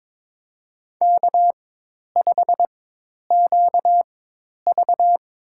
Morse code, heard as a keyed tone, about 22 wpm, spaced as for 10 wpm, 700 Hz. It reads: K5QV
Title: USA Call Signs - 4x Character Spacing - 22wpm